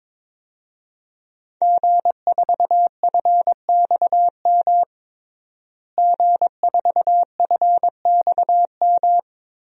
Morse code, heard as a keyed tone, about 22 wpm, 700 Hz. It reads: G4FXM G4FXM
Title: DXCC-100 Call Signs - 5 Characters - Repeated 2x - 22wpm